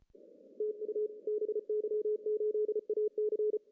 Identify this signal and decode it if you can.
Morse code, heard as a keyed tone, about 34 wpm, 410 Hz, with radio noise. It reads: TU BY8AC
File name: CW / cqww-cw-k5tr